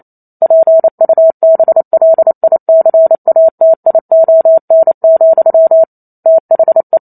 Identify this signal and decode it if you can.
Morse code, heard as a keyed tone, about 28 wpm, 650 Hz. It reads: PUBLICATION, THE